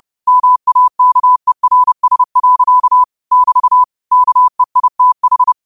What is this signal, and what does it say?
Morse code, heard as a keyed tone, about 30 wpm, 1000 Hz. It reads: MAKERS. <BT> KEITH
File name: qotdfiles / 30wpm